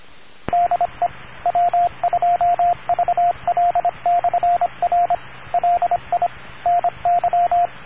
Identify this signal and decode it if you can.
Morse code, heard as a keyed tone, about 26 wpm, 700 Hz, with radio noise. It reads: DE W2VL/R LI NY